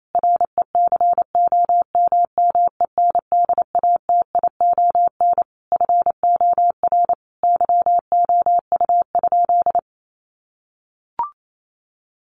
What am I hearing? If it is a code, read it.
Morse code, heard as a keyed tone, about 28 wpm, 700 Hz. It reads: RECOMMENDATION FOR YOU?
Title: Sentences from Top 1400 Words 28wpm